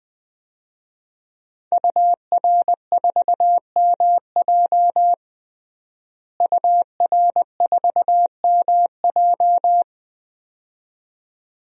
Morse code, heard as a keyed tone, about 20 wpm, 700 Hz. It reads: UR4MJ UR4MJ
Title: DXCC-100 Call Signs - 5 Characters - Repeated 2x - 20wpm